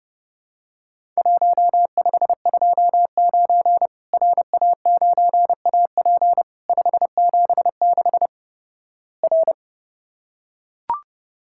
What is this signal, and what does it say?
Morse code, heard as a keyed tone, about 32 wpm, 700 Hz. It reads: 1529 RA9AP 576 R
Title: Worked All Europe DX Contest - QTC Exchange - 30wpm